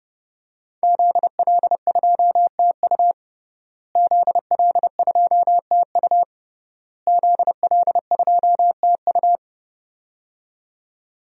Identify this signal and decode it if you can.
Morse code, heard as a keyed tone, about 30 wpm, 700 Hz. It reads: ZL2TU ZL2TU ZL2TU
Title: DXCC-100 Call Signs - 5 Characters - Repeated 3x - 30wpm